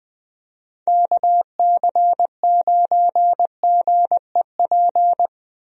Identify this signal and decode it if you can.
Morse code, heard as a keyed tone, about 20 wpm, 700 Hz. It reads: KC9GEP
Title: USA Call Signs - 6 Characters - Repeated 2x - 20wpm